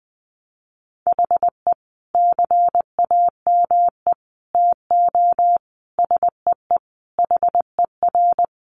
Morse code, heard as a keyed tone, about 20 wpm, 700 Hz. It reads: HE CAME TO SEE HER
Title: Sentences from Top 1300 Words 20wpm